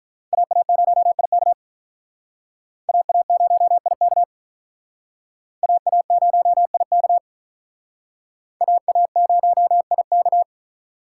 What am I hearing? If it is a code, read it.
Morse code, heard as a keyed tone, about 44 wpm, 700 Hz. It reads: AA0IK AA0IK AA0IK AA0IK